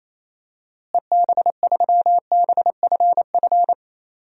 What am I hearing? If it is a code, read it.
Morse code, heard as a keyed tone, about 27 wpm, 700 Hz. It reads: EB3BFF
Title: DXCC-100 Call Signs - 6 Characters - Repeated 3x - 28wpm